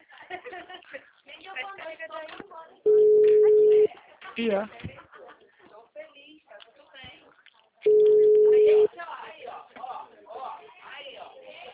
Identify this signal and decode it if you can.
Ringback tone (Europe, 425 Hz), with speech.